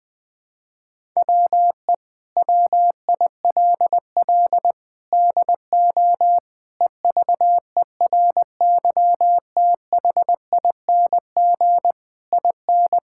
Morse code, heard as a keyed tone, about 20 wpm, 700 Hz. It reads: WE WILL DO EVERYTHING IN
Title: Sets of 8 Words - 20wpm